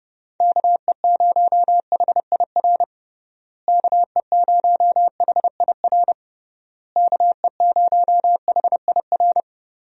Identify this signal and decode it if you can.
Morse code, heard as a keyed tone, about 31 wpm, 700 Hz. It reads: KE0HIR KE0HIR KE0HIR
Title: USA Call Signs - 6 Characters - Repeated 3x - 30wpm